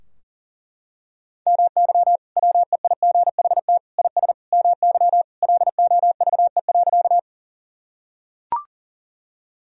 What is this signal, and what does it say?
Morse code, heard as a keyed tone, about 40 wpm, 700 Hz. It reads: MY WEIGHT IS MY LOVE.
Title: Sentences from Top 1000 Words - Review Part 3 - 40wpm